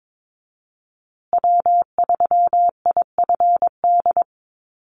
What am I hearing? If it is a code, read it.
Morse code, heard as a keyed tone, about 22 wpm, 700 Hz. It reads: W3IFD
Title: USA Call Signs Repeated 2x 22wpm